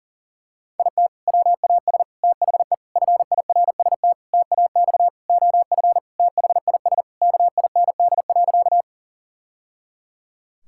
Morse code, heard as a keyed tone, about 40 wpm, 700 Hz. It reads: IT WAS THE FIRST TAX OF THIS KIND.
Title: Sentences from Top 1000 Words - Review Part 5 - 40wpm